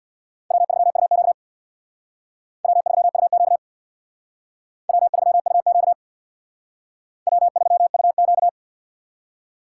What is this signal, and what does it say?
Morse code, heard as a keyed tone, about 64 wpm, 700 Hz. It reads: W3UQ W3UQ W3UQ W3UQ